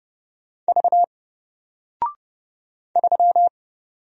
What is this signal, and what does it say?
Morse code, heard as a keyed tone, about 30 wpm, 700 Hz. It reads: V 3
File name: Single-Letter-Number-30wpm